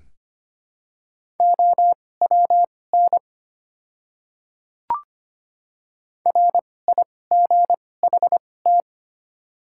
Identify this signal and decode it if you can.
Morse code, heard as a keyed tone, about 25 wpm, spaced as for 18 wpm, 700 Hz. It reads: OWN RIGHT
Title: 2x Character Spacing - Top 300 Words - 25wpm